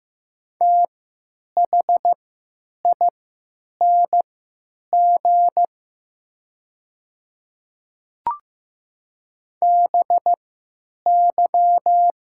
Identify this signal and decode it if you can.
Morse code, heard as a keyed tone, about 15 wpm, spaced as for 9 wpm, 700 Hz. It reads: THING BY
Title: 3x Character Spacing - Top 100 Words - 15wpm